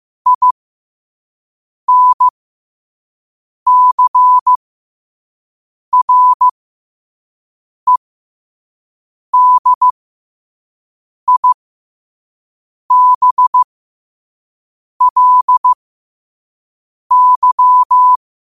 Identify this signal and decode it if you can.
Morse code, heard as a keyed tone, about 15 wpm, spaced as for 5 wpm, 1000 Hz. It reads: INCREDIBLY
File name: 2026-03-16-5wpm